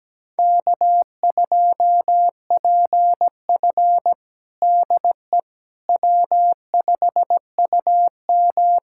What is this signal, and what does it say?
Morse code, heard as a keyed tone, about 17 wpm, 700 Hz. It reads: K2PF DE W5UM
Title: USA Callsign DE Callsign - 4 Characters - 17wpm